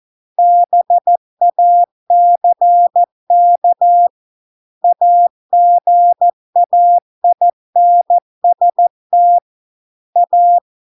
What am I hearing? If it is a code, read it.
Morse code, heard as a keyed tone, about 14 wpm, 700 Hz. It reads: BACK AGAINST A